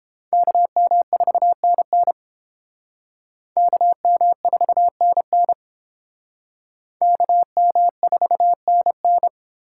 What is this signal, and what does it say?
Morse code, heard as a keyed tone, about 30 wpm, 700 Hz. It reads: KM4NN KM4NN KM4NN